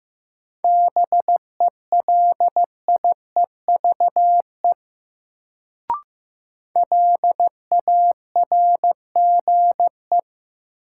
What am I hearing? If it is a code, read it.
Morse code, heard as a keyed tone, about 15 wpm, 700 Hz. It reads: BELIEVE LARGE
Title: Sets of 4 Words - New Word Drill - 15wpm